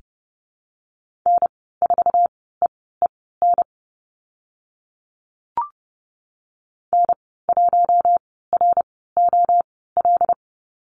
Morse code, heard as a keyed tone, about 30 wpm, spaced as for 17 wpm, 700 Hz. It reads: N4EEN N1ROL
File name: Lesson 15g - Number 5 - Call Signs 30wpm